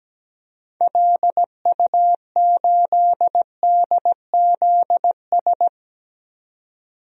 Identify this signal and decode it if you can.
Morse code, heard as a keyed tone, about 17 wpm, 700 Hz. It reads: LU8DZS
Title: DXCC-100 Call Signs - 6 Characters - 17wpm